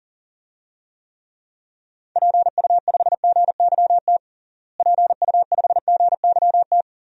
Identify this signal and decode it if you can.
Morse code, heard as a keyed tone, about 37 wpm, 700 Hz. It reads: PU5GYT PU5GYT
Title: DXCC-100 Call Signs - 6 Characters - Repeated 2x - 40wpm